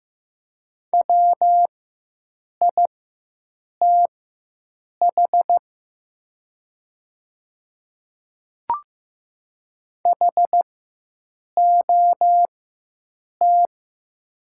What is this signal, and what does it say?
Morse code, heard as a keyed tone, about 15 wpm, spaced as for 7 wpm, 700 Hz. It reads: WITH HOT
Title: 4x Character Spacing - Top 100 Words - 15wpm